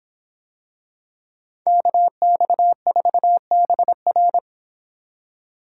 Morse code, heard as a keyed tone, about 25 wpm, 700 Hz. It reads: KX4BR